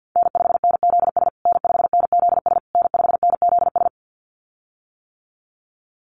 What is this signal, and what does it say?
Morse code, heard as a keyed tone, about 48 wpm, 700 Hz. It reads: N5NZS N5NZS N5NZS
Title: USA Call Signs Repeated 3x 50wpm